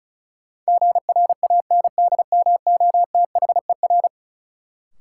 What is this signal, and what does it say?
Morse code, heard as a keyed tone, about 36 wpm, 700 Hz. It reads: GRANDMOTHER
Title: Sentences from Top 1200 Words - New Word Drill Alt - 35wpm